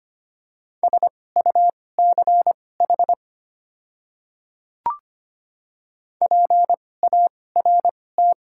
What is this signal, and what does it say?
Morse code, heard as a keyed tone, about 25 wpm, spaced as for 18 wpm, 700 Hz. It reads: SUCH PART
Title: 2x Character Spacing - Top 300 Words - 25wpm